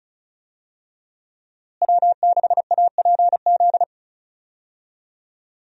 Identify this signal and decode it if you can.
Morse code, heard as a keyed tone, about 36 wpm, 700 Hz. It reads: W6APZ
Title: USA Call Signs - 5 Characters - Repeated 3x - 35wpm